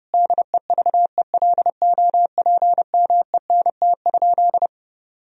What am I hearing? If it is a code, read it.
Morse code, heard as a keyed tone, about 30 wpm, 700 Hz. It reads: DEVELOPMENT?
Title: Sentences from Top 1000 Words - Review Part 4 - 30wpm